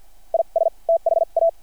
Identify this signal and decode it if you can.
Morse code, heard as a keyed tone, about 52 wpm, spaced as for 40 wpm, 660 Hz. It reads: IS THA
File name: archived / CRSnet